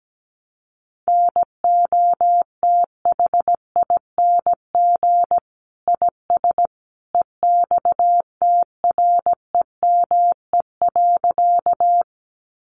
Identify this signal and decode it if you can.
Morse code, heard as a keyed tone, about 17 wpm, 700 Hz. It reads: NOTHING IS EXTREME.
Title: Sentences from Top 1300 Words 17wpm